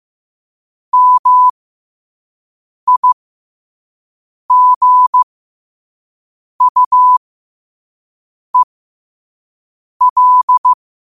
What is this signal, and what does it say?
Morse code, heard as a keyed tone, about 15 wpm, spaced as for 5 wpm, 1000 Hz. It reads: MIGUEL
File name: qotdfiles / 5wpm